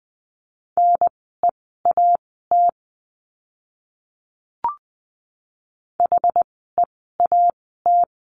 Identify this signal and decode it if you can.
Morse code, heard as a keyed tone, about 20 wpm, spaced as for 15 wpm, 700 Hz. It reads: NEAT HEAT
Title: 061 Lesson 11f - Letter h - Words 20wpm